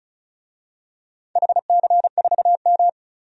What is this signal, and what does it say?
Morse code, heard as a keyed tone, about 36 wpm, 700 Hz. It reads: HC4M
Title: DXCC-100 Call Signs - 4 Characters - 35wpm